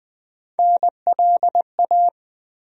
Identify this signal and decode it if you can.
Morse code, heard as a keyed tone, about 20 wpm, 700 Hz. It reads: NLA
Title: USA Call Sign Postfixes - 3 Letters 20wpm